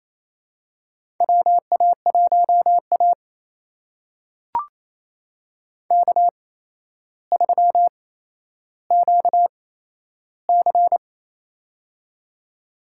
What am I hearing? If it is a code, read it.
Morse code, heard as a keyed tone, about 28 wpm, spaced as for 12 wpm, 700 Hz. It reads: WA1A K3QC
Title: USA Call Signs - 4 Characters - 8x Character Spacing - 28wpm